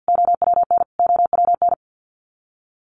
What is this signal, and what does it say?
Morse code, heard as a keyed tone, about 48 wpm, 700 Hz. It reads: OWN OWN
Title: Top 300 Words Repeated 50wpm